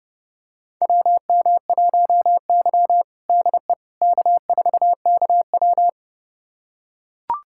Morse code, heard as a keyed tone, about 30 wpm, 700 Hz. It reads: WM1Y DE K4KW
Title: USA Callsign DE Callsign - 4 Characters - 30wpm